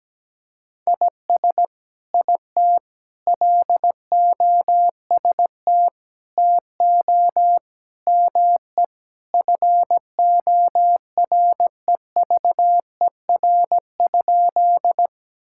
Morse code, heard as a keyed tone, about 17 wpm, 700 Hz. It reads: IS IT LOST TO ME FOREVER?
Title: Sentences from Top 1600 Words 17wpm